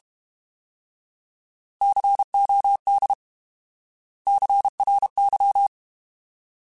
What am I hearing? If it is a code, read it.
Morse code, heard as a keyed tone, about 32 wpm, 800 Hz. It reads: COD CRY
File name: Alphabetical / Alphabetical mp3s / Cs @30